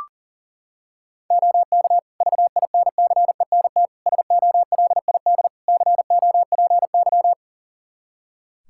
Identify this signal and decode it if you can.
Morse code, heard as a keyed tone, about 40 wpm, 700 Hz. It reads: OK VINCENT SOLID COPY
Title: Practice QSO - Exchange 3 - More Info - 40wpm